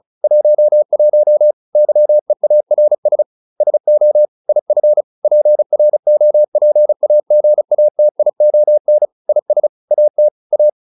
Morse code, heard as a keyed tone, about 34 wpm, 600 Hz. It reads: 11 YEARS SO IF PROPAGATION IS AT A